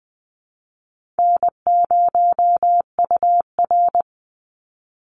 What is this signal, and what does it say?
Morse code, heard as a keyed tone, about 20 wpm, 700 Hz. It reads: N0UR
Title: Easy Call Signs - 20wpm